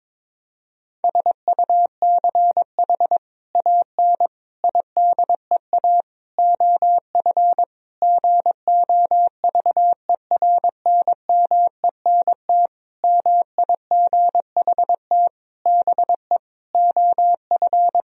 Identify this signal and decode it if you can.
Morse code, heard as a keyed tone, about 22 wpm, 700 Hz. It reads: SUCH AN IDEA OF GOVERNMENT MIGHT BE OF
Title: Sentences from Top 1000 Words - Review Part 3 - 22wpm